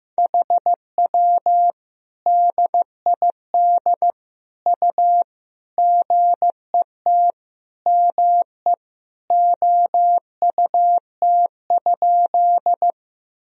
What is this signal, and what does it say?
Morse code, heard as a keyed tone, about 15 wpm, 700 Hz. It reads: HW DID U GET ME OUT?
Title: Sentences from Top 200 Words and Abbreviations 15wpm